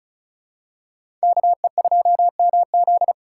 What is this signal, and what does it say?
Morse code, heard as a keyed tone, about 36 wpm, 700 Hz. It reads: KE2MZ
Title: USA Call Signs - 3x Character Spacing - 35wpm